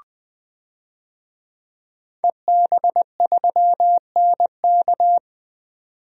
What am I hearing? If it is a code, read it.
Morse code, heard as a keyed tone, about 21 wpm, 700 Hz. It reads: EB3NK
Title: DXCC-100 Call Signs - 5 Characters - 20wpm